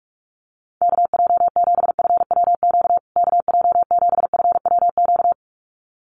Morse code, heard as a keyed tone, about 44 wpm, 700 Hz. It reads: KJ7FWQ KJ7FWQ
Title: USA Call Signs Repeated 2x 45wpm